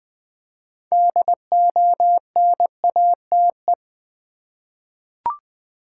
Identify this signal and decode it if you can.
Morse code, heard as a keyed tone, about 20 wpm, 700 Hz. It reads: DONATE
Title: 6 Letter Words - First 3 Letters Given - 20wpm